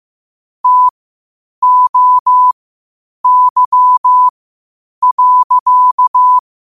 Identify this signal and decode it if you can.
Morse code, heard as a keyed tone, about 15 wpm, spaced as for 8 wpm, 1000 Hz. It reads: TOY.
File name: 8wpm